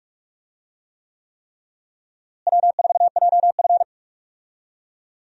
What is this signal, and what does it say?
Morse code, heard as a keyed tone, about 46 wpm, 700 Hz. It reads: W4JF
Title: USA Call Signs - 4 Characters - Repeated 3x - 45wpm